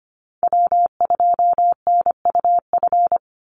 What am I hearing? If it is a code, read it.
Morse code, heard as a keyed tone, about 25 wpm, 700 Hz. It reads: W2NUF
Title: USA Call Signs 25wpm